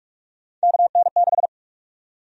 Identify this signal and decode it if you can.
Morse code, heard as a keyed tone, about 43 wpm, 700 Hz. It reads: KN6
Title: USA Call Sign Prefixes - 45wpm